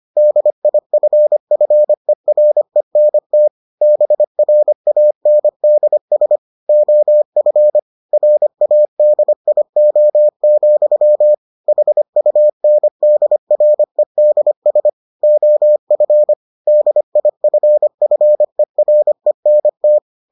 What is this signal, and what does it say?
Morse code, heard as a keyed tone, about 25 wpm, 600 Hz. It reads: DIFFERENT BRANDS OF RADIO, HUNDREDS OF DIFFERENT